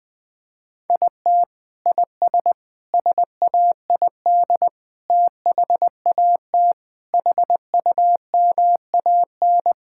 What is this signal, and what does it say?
Morse code, heard as a keyed tone, about 21 wpm, 700 Hz. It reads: IT IS SAID THAT HUMAN
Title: Sentences from Top 1000 Words - Review Part 6 - 20wpm